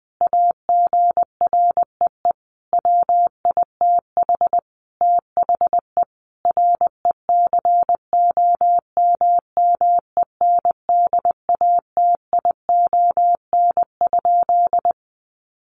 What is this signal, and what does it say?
Morse code, heard as a keyed tone, about 20 wpm, 700 Hz. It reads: AGREE WITH THE RECOMMENDATION?
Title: Sentences from Top 1400 Words 20wpm